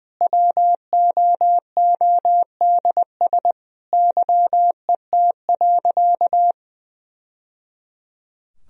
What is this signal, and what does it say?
Morse code, heard as a keyed tone, about 20 wpm, 700 Hz. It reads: WOODS YET.
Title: Sentences from Top 1000 Words - Review Part 2 - 20wpm